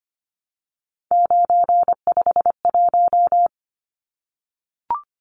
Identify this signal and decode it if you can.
Morse code, heard as a keyed tone, about 25 wpm, 690 Hz. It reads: 951
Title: Numbers - 3-Digit 25wpm